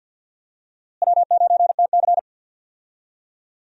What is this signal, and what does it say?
Morse code, heard as a keyed tone, about 47 wpm, 700 Hz. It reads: W9TC
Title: USA Call Signs - 4 Characters - Repeated 2x - 50wpm